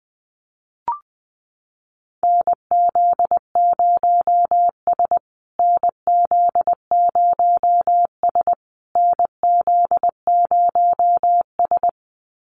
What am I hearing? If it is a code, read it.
Morse code, heard as a keyed tone, about 20 wpm, 700 Hz. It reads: NZ0S NZ0S NZ0S
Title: USA Call Signs Repeated 3x 20wpm